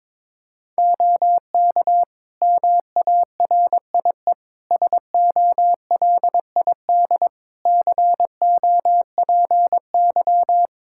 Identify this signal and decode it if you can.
Morse code, heard as a keyed tone, about 22 wpm, 700 Hz. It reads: OK MARIE SOLID COPY
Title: Practice QSO - Exchange 3 - More Info - 22wpm